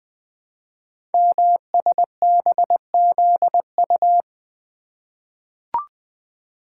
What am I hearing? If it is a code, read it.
Morse code, heard as a keyed tone, about 20 wpm, 700 Hz. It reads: MSBZU
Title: Code Groups - 5 Letters - 20wpm